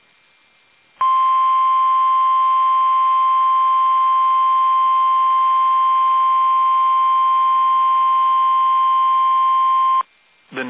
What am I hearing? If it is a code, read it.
Emergency Alert System (SAME) tones (the 1050 Hz attention tone), with speech. It is heard from 1.0 to 10.1 s.
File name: eas / KUSO / KUSO